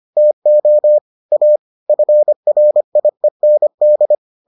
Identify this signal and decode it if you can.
Morse code, heard as a keyed tone, about 25 wpm, 600 Hz. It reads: TO A FRIEND